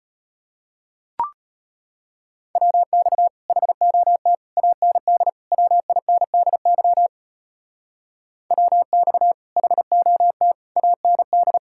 Morse code, heard as a keyed tone, about 30 wpm, 700 Hz. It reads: WX HOT AND WINDY WX HOT AND
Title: Speed-Racing Common QSO Phrases Encore - 25wpm